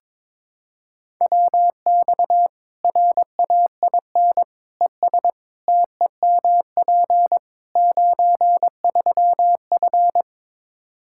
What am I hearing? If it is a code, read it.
Morse code, heard as a keyed tone, about 21 wpm, 700 Hz. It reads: WX RAIN ES TEMP 93F
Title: Practice QSO - Exchange 2 - Info - 22wpm